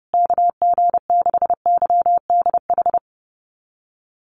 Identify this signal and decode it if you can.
Morse code, heard as a keyed tone, about 28 wpm, 700 Hz. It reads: KG6YDH
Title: USA Call Signs Repeated 3x 30wpm